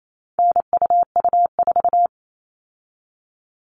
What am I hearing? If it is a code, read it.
Morse code, heard as a keyed tone, about 28 wpm, 700 Hz. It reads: NUU4
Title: All Cut Numbers - 4-Digit 28wpm